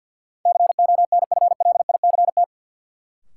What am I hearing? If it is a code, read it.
Morse code, heard as a keyed tone, about 52 wpm, 700 Hz. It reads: CONFLICT
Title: Sentences from Top 1300 Words - New Word Drill Alt - 50wpm